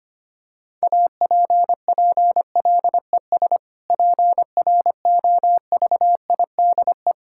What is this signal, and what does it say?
Morse code, heard as a keyed tone, about 25 wpm, 700 Hz. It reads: APPLES PROVIDE
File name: practice / 25wpm / Difficult Sentences with Hints - 25wpm